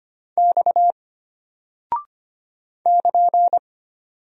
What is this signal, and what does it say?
Morse code, heard as a keyed tone, about 25 wpm, 700 Hz. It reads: X <KN>
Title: Single Character (Letter:Number:Punctuation:Prosign) Alt 25wpm